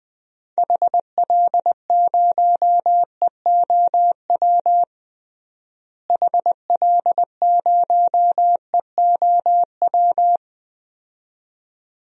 Morse code, heard as a keyed tone, about 20 wpm, 700 Hz. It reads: HL0EOW HL0EOW
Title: DXCC-100 Call Signs - 6 Characters - Repeated 2x - 20wpm